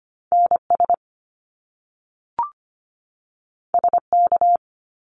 Morse code, heard as a keyed tone, about 25 wpm, 700 Hz. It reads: NS SK
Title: Canadian Provinces and Territories 25wpm